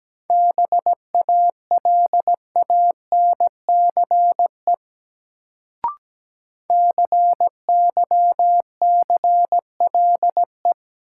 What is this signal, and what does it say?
Morse code, heard as a keyed tone, about 17 wpm, 700 Hz. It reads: BALANCE CYCLE
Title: Sentences from Top 900 Words - New Word Drill - 17wpm